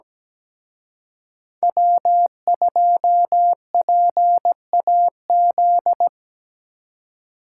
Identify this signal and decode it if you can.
Morse code, heard as a keyed tone, about 20 wpm, 700 Hz. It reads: W2PAZ
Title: Speed Racing - USA Call Signs 17wpm